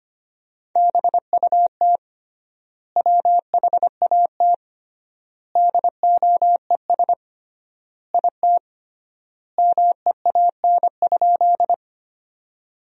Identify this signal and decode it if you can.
Morse code, heard as a keyed tone, about 25 wpm, 700 Hz. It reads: BUT WHAT DOES IT MEAN?
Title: 3x Word-Spacing - Sentences from Top 200 Words - 25wpm